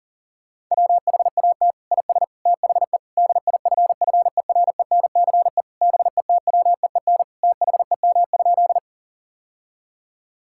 Morse code, heard as a keyed tone, about 40 wpm, 700 Hz. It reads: WHAT IS THE DIFFERENCE BETWEEN THEM?
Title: Sentences from Top 1000 Words - Review Part 5 - 40wpm